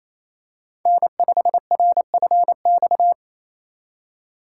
Morse code, heard as a keyed tone, about 28 wpm, 700 Hz. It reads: N5RFX
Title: USA Call Signs - 4x Character Spacing - 28wpm